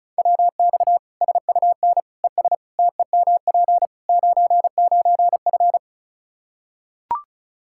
Morse code, heard as a keyed tone, about 36 wpm, 700 Hz. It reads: WX SUN ES TEMP 99F
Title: Practice QSO - Exchange 2 - Info - 35wpm